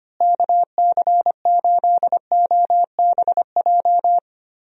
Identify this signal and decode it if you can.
Morse code, heard as a keyed tone, about 25 wpm, 700 Hz. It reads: KC8OBJ
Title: USA Call Signs - 6 Characters - Repeated 2x - 25wpm